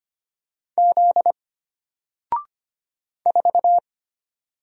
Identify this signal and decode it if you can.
Morse code, heard as a keyed tone, about 25 wpm, 700 Hz. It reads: Z 4
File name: Single-Letter-Number-25wpm